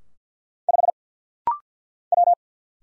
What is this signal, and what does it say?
Morse code, heard as a keyed tone, about 50 wpm, 700 Hz. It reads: 5 W
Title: Single Letter-Number - Rapid-Fire 50wpm